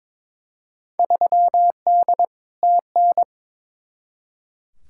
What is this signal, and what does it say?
Morse code, heard as a keyed tone, about 22 wpm, 700 Hz. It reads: 3D TN
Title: Field Day Exchanges Encore - 22wpm